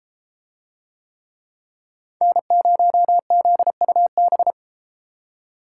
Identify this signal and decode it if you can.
Morse code, heard as a keyed tone, about 32 wpm, 700 Hz. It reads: N0ZUB